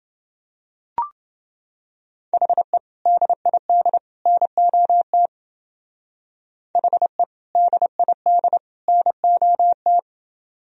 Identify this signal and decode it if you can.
Morse code, heard as a keyed tone, about 28 wpm, 700 Hz. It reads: HE DID NOT HE DID NOT